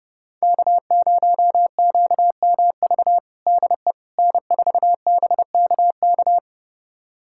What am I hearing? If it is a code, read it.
Morse code, heard as a keyed tone, about 30 wpm, 700 Hz. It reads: K0QMV DE N4BKK
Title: USA Callsign DE Callsign - 5 Characters - 30wpm